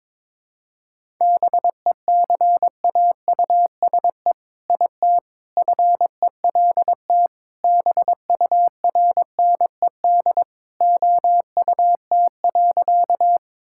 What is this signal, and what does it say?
Morse code, heard as a keyed tone, about 22 wpm, 700 Hz. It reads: BECAUSE IT FELT BURNED OUT.
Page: morse-code-ninja-practice-22wpm